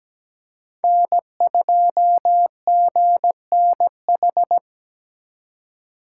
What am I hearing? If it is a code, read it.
Morse code, heard as a keyed tone, about 20 wpm, 700 Hz. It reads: N2GNH
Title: Speed Racing - USA 1x3 Call Signs 17wpm